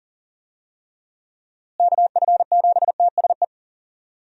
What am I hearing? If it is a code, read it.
Morse code, heard as a keyed tone, about 40 wpm, 700 Hz. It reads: KF7TSE
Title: USA Call Signs - 6 Characters - Repeated 3x - 40wpm